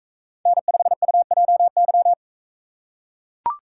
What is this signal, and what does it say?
Morse code, heard as a keyed tone, about 42 wpm, 700 Hz. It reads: N5UJY